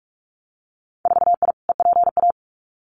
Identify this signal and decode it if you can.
Morse code, heard as a keyed tone, about 44 wpm, 700 Hz. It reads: 4I EPA
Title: Winter Field Day Exchanges Encore 45wpm